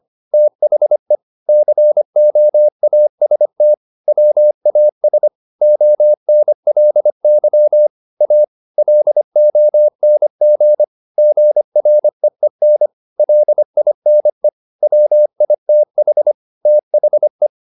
Morse code, heard as a keyed tone, about 25 wpm, 600 Hz. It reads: THE COAST WAS ONLY A LONG GREEN LINE WITH THE
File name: the old man and the sea part 1 25wpm